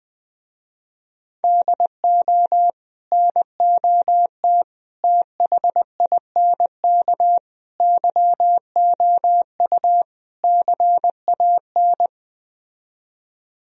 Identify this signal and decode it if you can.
Morse code, heard as a keyed tone, about 20 wpm, 700 Hz. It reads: DO NOT THINK YOU CAN
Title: Speed-Racing - Sets of 5 Words 17wpm